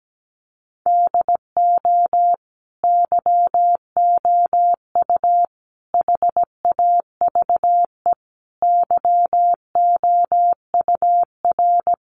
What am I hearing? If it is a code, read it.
Morse code, heard as a keyed tone, about 17 wpm, 700 Hz. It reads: DO YOU HAVE YOUR
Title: Sentences from Top 1400 Words 17wpm